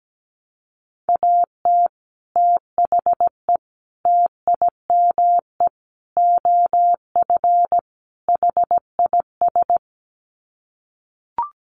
Morse code, heard as a keyed tone, about 17 wpm, 700 Hz. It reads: AT THE TIME OF HIS
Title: Sets of 5 Words Encore2 17wpm